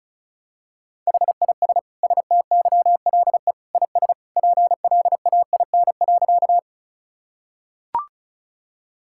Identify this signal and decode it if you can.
Morse code, heard as a keyed tone, about 34 wpm, 700 Hz. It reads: HIS STYLE IS PLAIN.
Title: Sentences from Top 1000 Words - Review Part 5 - 35wpm